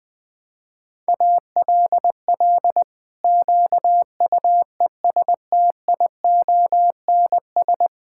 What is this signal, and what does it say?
Morse code, heard as a keyed tone, about 21 wpm, 700 Hz. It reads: ALL QUESTIONS
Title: Pangram Challenge v2 - 20wpm